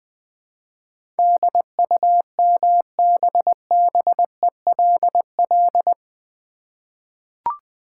Morse code, heard as a keyed tone, about 21 wpm, 700 Hz. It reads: DUMBBELL
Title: Easy Words with Hints - 20wpm